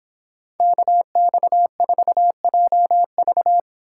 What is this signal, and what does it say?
Morse code, heard as a keyed tone, about 26 wpm, 700 Hz. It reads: KX4JV